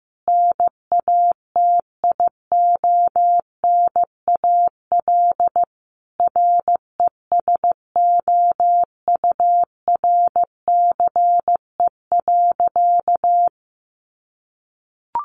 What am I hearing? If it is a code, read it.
Morse code, heard as a keyed tone, about 15 wpm, 700 Hz. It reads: NATIONAL RESOURCE.
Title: Sentences from Top 1300 Words 15wpm